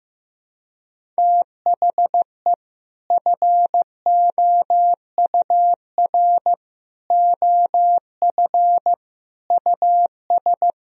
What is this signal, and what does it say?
Morse code, heard as a keyed tone, about 15 wpm, 700 Hz. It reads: THE FOUR OF US
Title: Sentences from Top 1000 Words - Review Part 1 - 15wpm